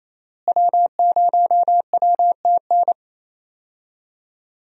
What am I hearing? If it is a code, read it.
Morse code, heard as a keyed tone, about 28 wpm, 700 Hz. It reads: W0WTN
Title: USA Call Signs - 5 Characters - Repeated 3x - 28wpm